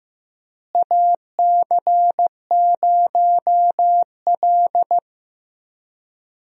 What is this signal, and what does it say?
Morse code, heard as a keyed tone, about 15 wpm, 700 Hz. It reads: AC0L
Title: USA Call Signs - 4 Characters - Repeated 3x - 15wpm